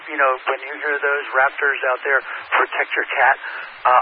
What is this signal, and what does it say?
Emergency Alert System (SAME) tones, heard as harsh buzzing bursts, with speech. It is heard from 0.6 to 3.7 s.